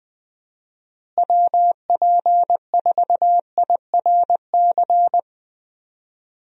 Morse code, heard as a keyed tone, about 21 wpm, 700 Hz. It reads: WP4IRC
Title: USA Call Signs - 6 Characters - Repeated 3x - 20wpm